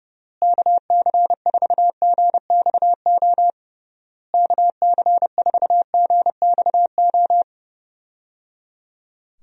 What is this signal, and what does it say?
Morse code, heard as a keyed tone, about 30 wpm, 700 Hz. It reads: KC4GXO KC4GXO
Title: USA Call Signs - 6 Characters - Repeated 3x - 30wpm